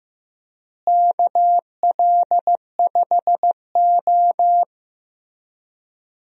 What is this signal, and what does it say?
Morse code, heard as a keyed tone, about 15 wpm, 700 Hz. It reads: KL5O
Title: USA Call Signs - 4 Characters - Repeated 3x - 15wpm